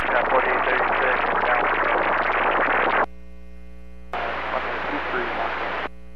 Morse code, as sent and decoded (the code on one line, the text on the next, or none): none